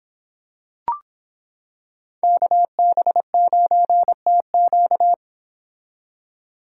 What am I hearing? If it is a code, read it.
Morse code, heard as a keyed tone, about 26 wpm, 700 Hz. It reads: KB9TQ